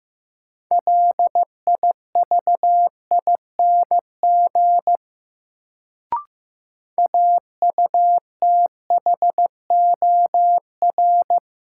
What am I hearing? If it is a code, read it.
Morse code, heard as a keyed tone, about 15 wpm, 700 Hz. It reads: LIVING AUTHOR
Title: Sets of 4 Words - New Word Drill - 15wpm